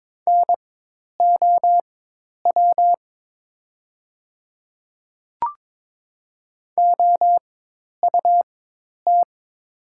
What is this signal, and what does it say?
Morse code, heard as a keyed tone, about 22 wpm, spaced as for 10 wpm, 700 Hz. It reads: NOW OUT
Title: 4x Character Spacing - Top 100 Words - 22wpm